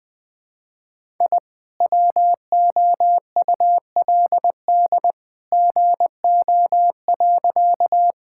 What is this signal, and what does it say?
Morse code, heard as a keyed tone, about 20 wpm, 700 Hz. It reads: I WOULD GO.
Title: Sentences with 2-Word Phrases - 20wpm